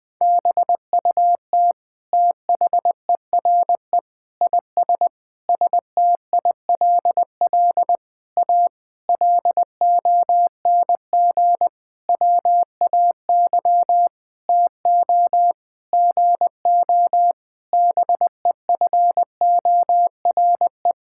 Morse code, heard as a keyed tone, about 21 wpm, 700 Hz. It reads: BUT THERE IS STILL A LONG WAY TO GO BEFORE
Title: Sentences with 9-Word Phrases - 20wpm